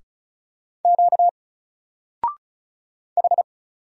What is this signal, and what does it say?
Morse code, heard as a keyed tone, about 36 wpm, 700 Hz. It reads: Q H
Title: Single Letter-Number Alt 35wpm